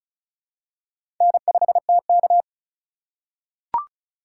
Morse code, heard as a keyed tone, about 34 wpm, 700 Hz. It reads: N5TK
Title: USA Call Signs - 4 Characters - 4x Character Spacing - 35wpm